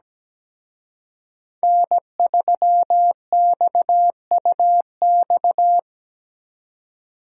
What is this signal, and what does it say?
Morse code, heard as a keyed tone, about 20 wpm, 700 Hz. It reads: N3XUX
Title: Speed Racing - USA 1x3 Call Signs 17wpm